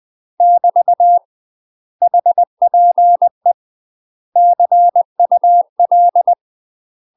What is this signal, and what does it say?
Morse code, heard as a keyed tone, about 20 wpm, 700 Hz. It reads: <BT> HPE CUL
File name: UK MASTER QSOs TRAINING/20WPM